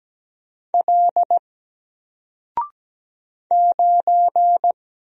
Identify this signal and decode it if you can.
Morse code, heard as a keyed tone, about 17 wpm, 700 Hz. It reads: L 9
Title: Single Letter-Number 17wpm